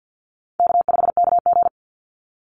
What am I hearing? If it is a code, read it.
Morse code, heard as a keyed tone, about 50 wpm, 700 Hz. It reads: K5KG
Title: Easy Call Signs - 50wpm